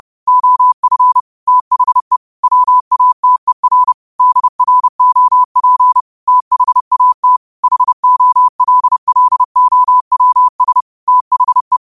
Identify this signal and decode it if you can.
Morse code, heard as a keyed tone, about 30 wpm, 1000 Hz. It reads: OF THE WATER DROP THAT HOLLOWS THE
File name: qotdfiles / 30wpm